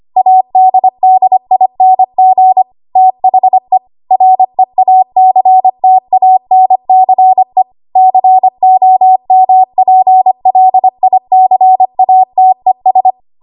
Morse code, heard as a keyed tone, about 25 wpm, 750 Hz. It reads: ADDING THE REACTANCE COMPLICATES
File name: Archive / 25 WPM / 25WPM